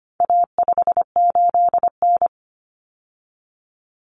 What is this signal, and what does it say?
Morse code, heard as a keyed tone, about 25 wpm, 700 Hz. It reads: A58N
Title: Common Cut Numbers - 4-Digit 25wpm